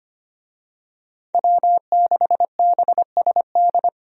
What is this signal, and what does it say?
Morse code, heard as a keyed tone, about 25 wpm, 700 Hz. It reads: W6BSD
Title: USA Call Signs - 5 Characters - Repeated 3x - 25wpm